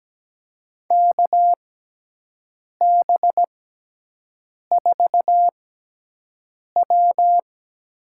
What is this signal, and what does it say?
Morse code, heard as a keyed tone, about 17 wpm, spaced as for 6 wpm, 700 Hz. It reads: KB4W
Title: USA Call Signs - 6 Characters - 6x Character Spacing - 17wpm